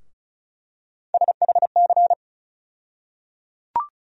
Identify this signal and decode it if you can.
Morse code, heard as a keyed tone, about 34 wpm, 700 Hz. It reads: SHC
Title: USA Call Sign Postfixes - 3 Letters 35wpm